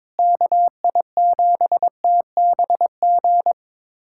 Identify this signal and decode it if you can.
Morse code, heard as a keyed tone, about 22 wpm, 700 Hz. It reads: KI7TBG
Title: USA Callsign DE Callsign - 6 Characters - 22wpm